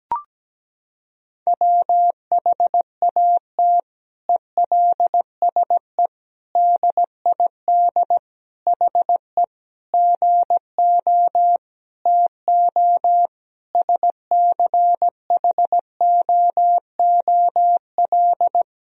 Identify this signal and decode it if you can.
Morse code, heard as a keyed tone, about 17 wpm, 700 Hz. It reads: WHAT ELSE DID HE GO TO SCHOOL
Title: Sentences from Top 1000 Words - Review Part 1 - 17wpm